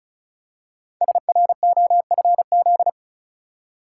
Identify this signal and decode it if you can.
Morse code, heard as a keyed tone, about 36 wpm, 700 Hz. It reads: SROFZ
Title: Code Groups - 5 Letters - 35wpm